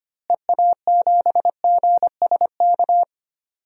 Morse code, heard as a keyed tone, about 25 wpm, 700 Hz. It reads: EA7GSK
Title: DXCC-100 Call Signs - 6 Characters - Repeated 3x - 25wpm